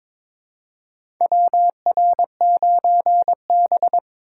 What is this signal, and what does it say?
Morse code, heard as a keyed tone, about 21 wpm, 700 Hz. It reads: WR9B
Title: USA Call Signs - 4 Characters - 3x Character Spacing - 22wpm